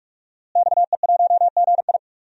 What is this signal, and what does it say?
Morse code, heard as a keyed tone, about 46 wpm, 700 Hz. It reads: XE1GI
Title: DXCC-100 Call Signs - 5 Characters - Repeated 2x - 45wpm